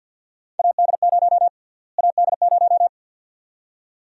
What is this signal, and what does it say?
Morse code, heard as a keyed tone, about 53 wpm, 700 Hz. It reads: AD0 AD0
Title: USA Call Sign Prefixes - Repeated 2x - 50wpm